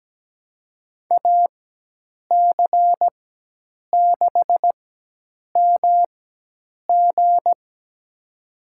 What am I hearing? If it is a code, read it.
Morse code, heard as a keyed tone, about 17 wpm, spaced as for 8 wpm, 700 Hz. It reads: AC6MG
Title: USA Call Signs - 5 Characters - 4x Character Spacing - 17wpm